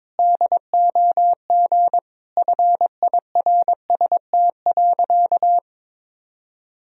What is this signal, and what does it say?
Morse code, heard as a keyed tone, about 21 wpm, 700 Hz. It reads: DOG FIRST.
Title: Sentences from Top 1000 Words - Review Part 2 - 22wpm